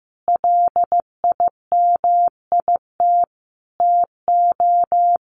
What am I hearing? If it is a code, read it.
Morse code, heard as a keyed tone, about 15 wpm, 700 Hz. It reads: LIMIT TO
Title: Sentences from Top 1500 Words 15wpm